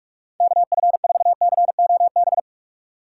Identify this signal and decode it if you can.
Morse code, heard as a keyed tone, about 47 wpm, 700 Hz. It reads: KF4COB
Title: USA Call Signs - 6 Characters - 4x Character Spacing - 45wpm